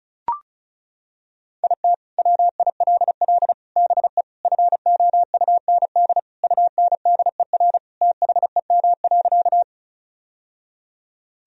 Morse code, heard as a keyed tone, about 34 wpm, 700 Hz. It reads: IT WILL BE FOUND UNDER THEM.
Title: Sentences from Top 1000 Words - Review Part 1 - 35wpm